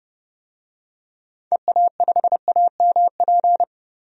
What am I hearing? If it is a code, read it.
Morse code, heard as a keyed tone, about 30 wpm, 700 Hz. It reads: EA5AMP
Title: DXCC-100 Call Signs - 6 Characters - 30wpm